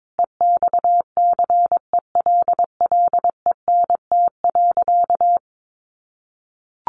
Morse code, heard as a keyed tone, about 22 wpm, 700 Hz. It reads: EXCELLENT.
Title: Sentences from Top 1000 Words 22wpm